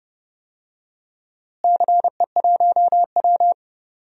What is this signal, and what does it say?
Morse code, heard as a keyed tone, about 30 wpm, 700 Hz. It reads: CE1W
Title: DXCC-100 Call Signs - 4 Characters - Repeated 2x - 30wpm